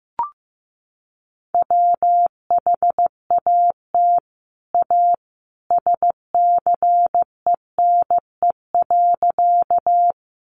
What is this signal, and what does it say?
Morse code, heard as a keyed tone, about 15 wpm, 700 Hz. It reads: WHAT A SCENE.
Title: Sentences from Top 1100 Words 15wpm